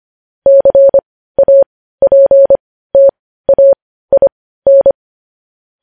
Morse code, heard as a keyed tone, about 25 wpm, spaced as for 15 wpm, 550 Hz. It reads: CAPTAIN
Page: news-headlines-in-morse-code-at-15-wpm